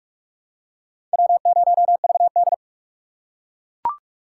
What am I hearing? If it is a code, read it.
Morse code, heard as a keyed tone, about 37 wpm, 700 Hz. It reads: W0VD
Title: Speed Racing - USA 1x2 Call Signs 30wpm